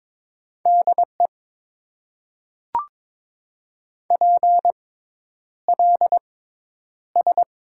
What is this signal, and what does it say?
Morse code, heard as a keyed tone, about 22 wpm, spaced as for 10 wpm, 700 Hz. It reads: DE PLS
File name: Top 90 QSO Elements - 6x Character Spacing - 22wpm